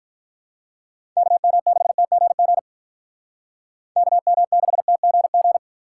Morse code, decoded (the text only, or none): KM6TGG KM6TGG